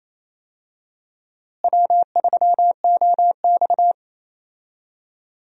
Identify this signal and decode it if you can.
Morse code, heard as a keyed tone, about 28 wpm, 700 Hz. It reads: W3OX
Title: USA Call Signs - 4 Characters - Repeated 2x - 28wpm